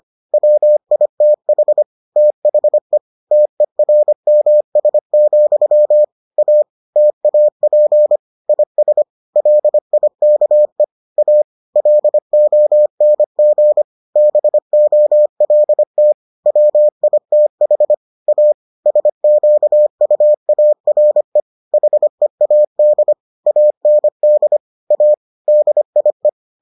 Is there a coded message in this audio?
Yes: Morse code from 0.3 to 26.3 s, heard as a keyed tone, about 25 wpm, 600 Hz. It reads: WITH THE TERMS, A TAP IS LIKE A LONG BOLT WITH A SQUARE HEAD AND A DIE